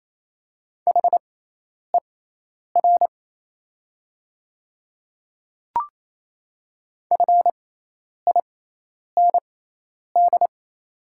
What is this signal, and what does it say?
Morse code, heard as a keyed tone, about 28 wpm, spaced as for 10 wpm, 700 Hz. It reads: HER FIND
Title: 6x Character Spacing - Top 100 Words - 28wpm